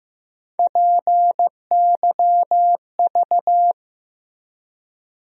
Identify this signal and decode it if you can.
Morse code, heard as a keyed tone, about 15 wpm, 700 Hz. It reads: PYV
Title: USA Call Sign Postfixes - 3 Letters 15wpm